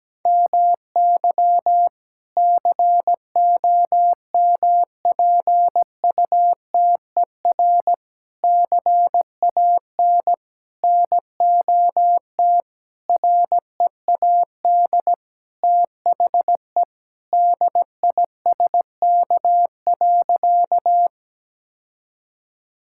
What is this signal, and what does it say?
Morse code, heard as a keyed tone, about 17 wpm, 700 Hz. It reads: MY COMPUTER CAN NOT READ THE DISK.
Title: Sentences from Top 1000 Words - Review Part 6 - 17wpm